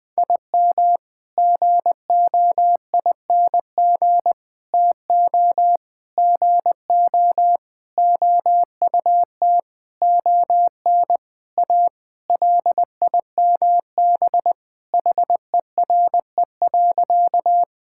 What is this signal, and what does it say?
Morse code, heard as a keyed tone, about 20 wpm, 700 Hz. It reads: IM GOING TO GO OUT ON A LIMB HERE.
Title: Sentences with 8-Word Phrases - Spoken Beginning - 20wpm